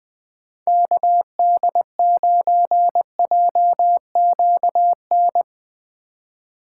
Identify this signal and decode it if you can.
Morse code, heard as a keyed tone, about 20 wpm, 700 Hz. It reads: KD9JQN
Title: USA Call Signs - 6 Characters - 8x Character Spacing - 20wpm